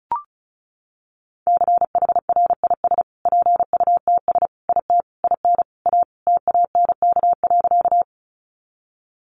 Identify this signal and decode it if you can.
Morse code, heard as a keyed tone, about 34 wpm, 700 Hz. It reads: CHRIS PUTS IT IN A TANK.
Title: Easy Phrases 35wpm